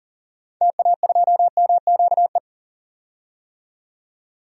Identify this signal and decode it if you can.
Morse code, heard as a keyed tone, about 40 wpm, 700 Hz. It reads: TA2MQE
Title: DXCC-100 Call Signs - 6 Characters - 40wpm